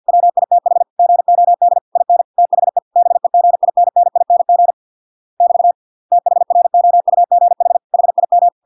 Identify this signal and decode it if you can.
Morse code, heard as a keyed tone, about 50 wpm, 700 Hz. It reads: WITH GOD IN THE BEGINNING <BT> THROUGH HIM